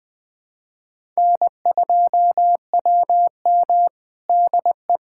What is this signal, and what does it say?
Morse code, heard as a keyed tone, about 21 wpm, 700 Hz. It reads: N2WM DE
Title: USA Callsign DE Callsign - 4 Characters - 20wpm